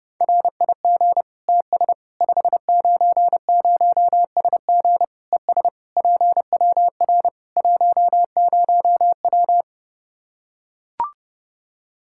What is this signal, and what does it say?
Morse code, heard as a keyed tone, about 30 wpm, 700 Hz. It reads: RIG TS 590SG ES PWR 10W
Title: Practice QSO - Exchange 2 - Info - 30wpm